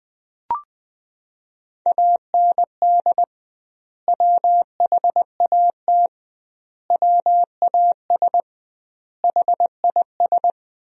Morse code, heard as a keyed tone, about 20 wpm, 700 Hz. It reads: AND WHAT WAS HIS
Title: 2x Word-Spacing - Sentences from Top 200 Words - 20wpm